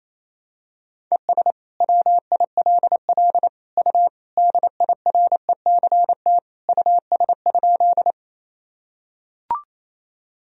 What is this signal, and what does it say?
Morse code, heard as a keyed tone, about 28 wpm, 700 Hz. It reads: ES WILL U DIRECT US?
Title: Sentences from Top 400 Words and Abbreviations 28wpm